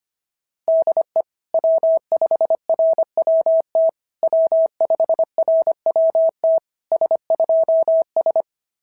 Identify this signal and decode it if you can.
Morse code, heard as a keyed tone, about 25 wpm, 650 Hz. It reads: DE W5RWT W5RWT S2S
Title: SOTA QSO Practice 25wpm